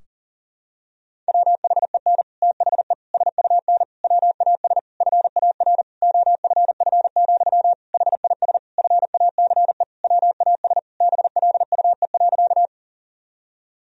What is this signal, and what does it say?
Morse code, heard as a keyed tone, about 38 wpm, 700 Hz. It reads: WHEN THE SUN WAS FAR OFF, HIS FACE WAS BLUE.
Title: Sentences from Top 1000 Words - Review Part 3 - 40wpm